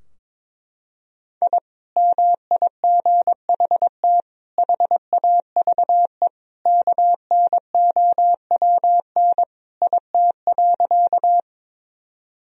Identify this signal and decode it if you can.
Morse code, heard as a keyed tone, about 21 wpm, 700 Hz. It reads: I MIGHT HAVE KNOWN IT.
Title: Sentences from Top 1000 Words - Review Part 3 - 22wpm